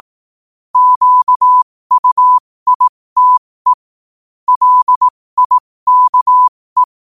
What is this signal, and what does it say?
Morse code, heard as a keyed tone, about 18 wpm, spaced as for 15 wpm, 1000 Hz. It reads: QUITE LIKE
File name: qotdfiles / 15wpm